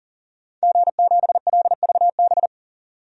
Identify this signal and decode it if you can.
Morse code, heard as a keyed tone, about 40 wpm, 700 Hz. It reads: G7LVB
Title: DXCC-100 Call Signs - 5 Characters - 40wpm